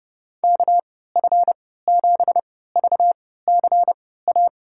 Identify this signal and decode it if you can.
Morse code, heard as a keyed tone, about 30 wpm, spaced as for 17 wpm, 700 Hz. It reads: KF7VCA
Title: USA Call Signs - 6 Characters - 3x Character Spacing - 30wpm